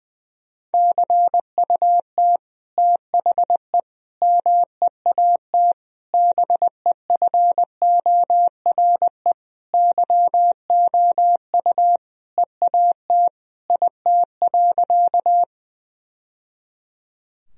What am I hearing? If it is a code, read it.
Morse code, heard as a keyed tone, about 20 wpm, 700 Hz. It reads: CUT THE MEAT BEFORE YOU EAT IT.
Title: Sentences from Top 1000 Words - Review Part 3 - 20wpm